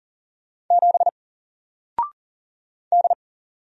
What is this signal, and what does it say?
Morse code, heard as a keyed tone, about 40 wpm, 700 Hz. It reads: 7 D
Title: Single Letter-Number Alt 40wpm